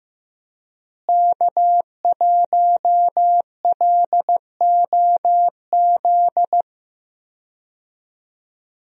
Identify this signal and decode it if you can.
Morse code, heard as a keyed tone, about 15 wpm, 700 Hz. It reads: K1LOZ
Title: USA Call Signs - 5 Characters - Repeated 3x - 15wpm